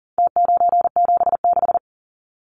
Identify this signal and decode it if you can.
Morse code, heard as a keyed tone, about 40 wpm, 700 Hz. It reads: T976
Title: Numbers - Contest Style - 4-Digit 40wpm